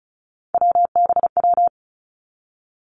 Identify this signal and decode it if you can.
Morse code, heard as a keyed tone, about 36 wpm, 700 Hz. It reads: WBW
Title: Three Challenging Letter-Number Combinations 35wpm